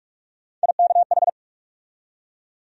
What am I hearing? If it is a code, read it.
Morse code, heard as a keyed tone, about 46 wpm, 700 Hz. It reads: IKH
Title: USA Call Sign Postfixes - 3 Letters 45wpm